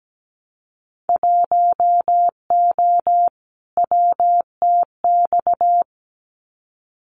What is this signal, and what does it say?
Morse code, heard as a keyed tone, about 17 wpm, 700 Hz. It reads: 1O WTX
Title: Winter Field Day Exchanges Encore 17wpm